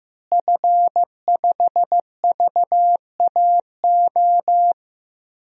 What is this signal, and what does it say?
Morse code, heard as a keyed tone, about 15 wpm, 700 Hz. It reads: F5VAO
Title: DXCC-100 Call Signs - 5 Characters - Repeated 3x - 15wpm